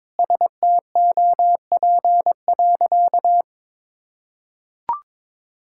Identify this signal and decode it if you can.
Morse code, heard as a keyed tone, about 21 wpm, 700 Hz. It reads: STOP.
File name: practice / 22wpm / Sentences with 2-Word Phrases - 22wpm